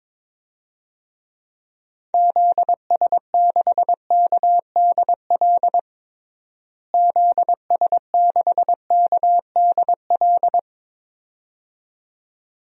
Morse code, heard as a keyed tone, about 22 wpm, 700 Hz. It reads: ZS6KDL ZS6KDL
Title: DXCC-100 Call Signs - 6 Characters - Repeated 2x - 22wpm